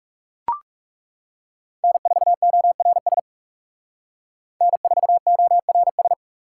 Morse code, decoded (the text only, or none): N4ORS N4ORS